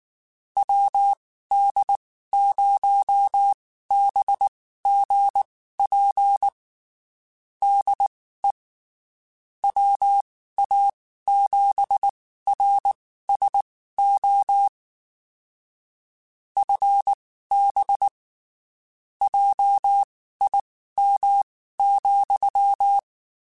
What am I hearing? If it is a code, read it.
Morse code, heard as a keyed tone, about 19 wpm, spaced as for 14 wpm, 800 Hz. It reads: WD0BGP DE WA7RSO FB JIM,
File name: (spaced) / QSOs Spaced-mp3s / WD0BGP 1 @18 sp